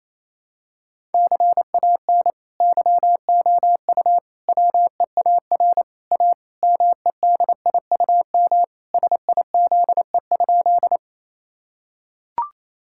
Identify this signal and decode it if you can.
Morse code, heard as a keyed tone, about 28 wpm, 700 Hz. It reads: CAN YOU WEAR A MEDIUM SIZE?
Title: Sentences from Top 1000 Words - Review Part 6 - 28wpm